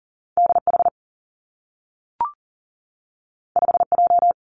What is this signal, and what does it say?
Morse code, heard as a keyed tone, about 40 wpm, 700 Hz. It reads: D H 5 J
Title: Two Challenging Letter-Number Combinations 40wpm